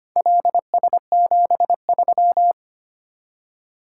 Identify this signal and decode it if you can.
Morse code, heard as a keyed tone, about 25 wpm, 700 Hz. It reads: LS73
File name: practice / 25wpm / Four Letter-Number Combinations - 25wpm